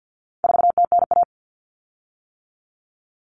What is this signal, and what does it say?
Morse code, heard as a keyed tone, about 50 wpm, 700 Hz. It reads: 4TNA
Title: Common Cut Numbers - 4-Digit 50wpm